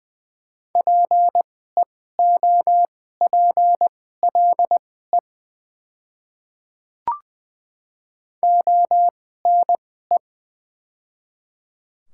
Morse code, heard as a keyed tone, about 20 wpm, spaced as for 14 wpm, 700 Hz. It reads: PEOPLE ONE
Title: 2x Character Spacing - Top 100 Words - 20wpm